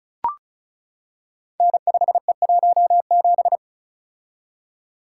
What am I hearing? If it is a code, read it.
Morse code, heard as a keyed tone, about 36 wpm, 700 Hz. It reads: N5E17
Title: Code Groups - 5 Letter:Numbers - 35wpm